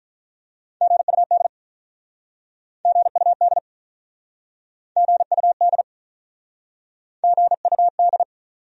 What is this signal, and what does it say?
Morse code, heard as a keyed tone, about 47 wpm, 700 Hz. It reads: GUD GUD GUD GUD